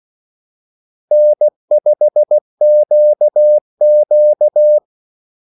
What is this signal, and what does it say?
Morse code, heard as a keyed tone, about 16 wpm, 600 Hz. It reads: N5QQ
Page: Kaldesignaler 16 wpm